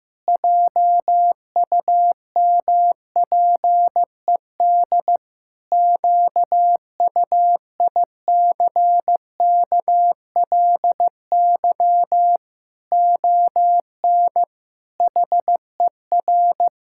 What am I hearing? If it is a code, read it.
Morse code, heard as a keyed tone, about 16 wpm, 700 Hz. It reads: JUMPED QUICKLY ON HER
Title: Pangram Challenge v2 - 15wpm